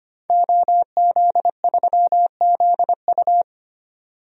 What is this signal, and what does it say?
Morse code, heard as a keyed tone, about 25 wpm, 700 Hz. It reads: OZ3ZU
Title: DXCC-100 Call Signs - 5 Characters - 25wpm